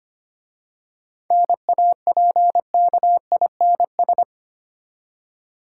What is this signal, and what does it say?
Morse code, heard as a keyed tone, about 25 wpm, 690 Hz. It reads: NAPKINS
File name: practice / 25wpm / Easy Words with Hints - 25wpm